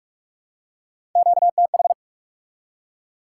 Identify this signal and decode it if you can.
Morse code, heard as a keyed tone, about 43 wpm, 700 Hz. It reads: QTH
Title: Top 90 QSO Elements 45wpm